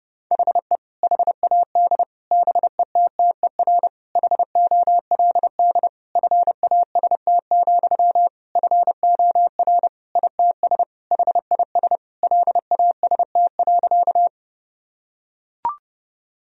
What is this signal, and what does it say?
Morse code, heard as a keyed tone, about 32 wpm, 700 Hz. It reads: HE HAD BETTER HOLD FAST, FOR ITS HIS LAST.
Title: Sentences from Top 1000 Words - Review Part 2 - 30wpm